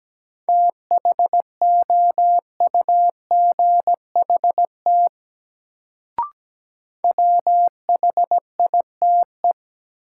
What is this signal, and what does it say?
Morse code, heard as a keyed tone, about 17 wpm, 700 Hz. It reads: THOUGHT WHITE
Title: Sentences from Top 300 Words - New Word Drill - 17wpm